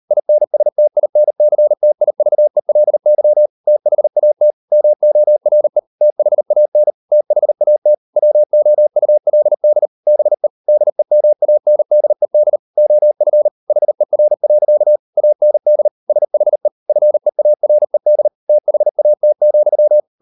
Morse code, heard as a keyed tone, about 38 wpm, 600 Hz. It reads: INSTINCTIVELY THAT MORE THAN THAT WOULD BE DEMANDED OF HER. AND SHE FEARED THAT,